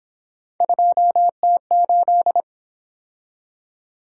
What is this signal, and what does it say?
Morse code, heard as a keyed tone, about 27 wpm, 700 Hz. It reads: 2T8
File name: Speed-Racing - Three Letter-Number Combinations - 22wpm